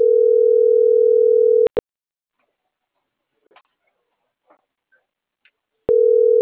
Ringback tone (North America), 440 + 480 Hz.
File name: CENTRAL NMP / out isabelponciano